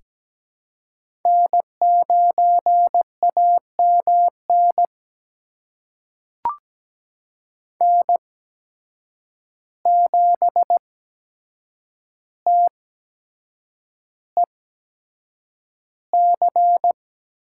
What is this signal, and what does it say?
Morse code, heard as a keyed tone, about 17 wpm, spaced as for 7 wpm, 700 Hz. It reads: N9AMN N7TEC
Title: USA Call Signs - 5 Characters - 8x Character Spacing - 17wpm